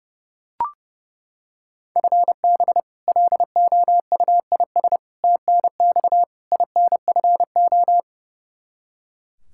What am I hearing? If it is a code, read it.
Morse code, heard as a keyed tone, about 30 wpm, 700 Hz. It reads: FB LOUIS TNX INFO
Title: Practice QSO - Exchange 3 - More Info - 30wpm